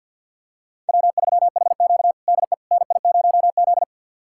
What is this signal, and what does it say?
Morse code, heard as a keyed tone, about 53 wpm, 690 Hz. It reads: W3HQ DE NI0Z
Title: USA Callsign DE Callsign - 4 Characters - 50wpm